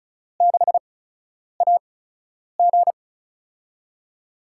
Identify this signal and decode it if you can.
Morse code, heard as a keyed tone, about 36 wpm, spaced as for 10 wpm, 700 Hz. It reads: 6AG
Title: USA Call Signs - 4 Characters - 8x Character Spacing - 35wpm